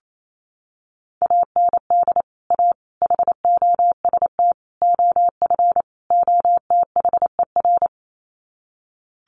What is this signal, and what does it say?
Morse code, heard as a keyed tone, about 28 wpm, 700 Hz. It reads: AND A HOST OF OTHER
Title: Sets of 5 Words Encore2 28wpm